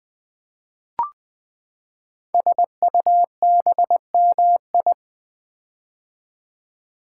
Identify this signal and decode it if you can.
Morse code, heard as a keyed tone, about 20 wpm, 700 Hz. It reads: SUBMI
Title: Build A Word - 5 Letters At A Time - 10-Letter Words - 20wpm